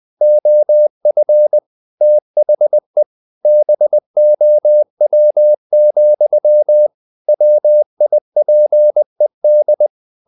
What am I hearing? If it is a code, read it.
Morse code, heard as a keyed tone, about 20 wpm, 600 Hz. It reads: OF THE BOW, WIPED
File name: the old man and the sea / the old man and the sea part 1 20wpm